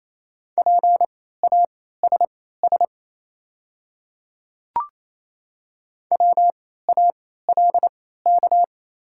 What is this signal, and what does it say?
Morse code, heard as a keyed tone, about 28 wpm, spaced as for 16 wpm, 700 Hz. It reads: PASS WALK
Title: 3x Character Spacing - Top 500 Words - 28wpm